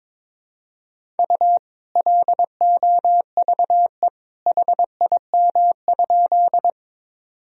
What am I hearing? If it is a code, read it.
Morse code, heard as a keyed tone, about 22 wpm, 700 Hz. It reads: U LOVE HIM?
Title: Sentences from Top 400 Words and Abbreviations 22wpm